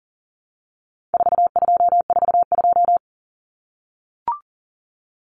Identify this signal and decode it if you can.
Morse code, heard as a keyed tone, about 40 wpm, 700 Hz. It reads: 4242
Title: Numbers - 4-Digit 40wpm